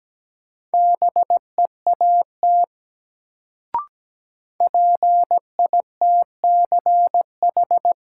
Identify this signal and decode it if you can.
Morse code, heard as a keyed tone, about 17 wpm, 700 Hz. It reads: BEAT PITCH
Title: Sentences from Top 1300 Words - New Word Drill - 17wpm